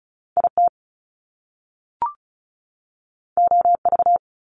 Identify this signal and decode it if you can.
Morse code, heard as a keyed tone, about 36 wpm, 700 Hz. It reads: IT OV
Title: Top Two Letter Combinations in Words 35wpm